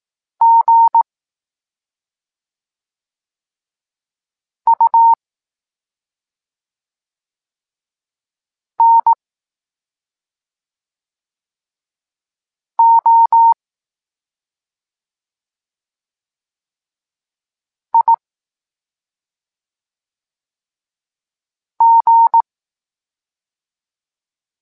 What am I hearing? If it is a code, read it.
Morse code, heard as a keyed tone, about 18 wpm, 920 Hz. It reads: G U N O I G
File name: Tekst-23 4